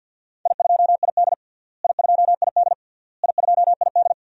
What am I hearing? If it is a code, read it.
Morse code, heard as a keyed tone, about 49 wpm, 700 Hz. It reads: I2ID I2ID I2ID
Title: DXCC-100 Call Signs - 4 Characters - Repeated 3x - 50wpm